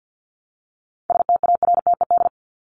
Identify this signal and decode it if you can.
Morse code, heard as a keyed tone, about 50 wpm, 700 Hz. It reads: STARTED
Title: ED Suffix - Top 100 Words with Base Words 50wpm